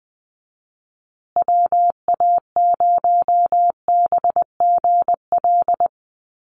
Morse code, heard as a keyed tone, about 20 wpm, 700 Hz. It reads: WA0BGL
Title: USA Call Signs Repeated 3x 20wpm